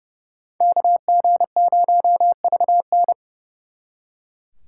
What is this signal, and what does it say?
Morse code, heard as a keyed tone, about 35 wpm, 700 Hz. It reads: KG0VN
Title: Speed Racing - USA 2x2 Call Signs 30wpm